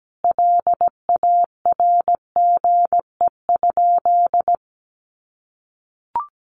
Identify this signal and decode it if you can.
Morse code, heard as a keyed tone, about 17 wpm, 700 Hz. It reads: LARGE?
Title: Sentences from Top 200 Words 17wpm